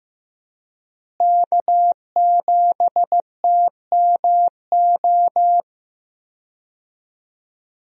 Morse code, heard as a keyed tone, about 15 wpm, 700 Hz. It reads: K7TMO
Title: USA Call Signs - 5 Characters - Repeated 2x - 15wpm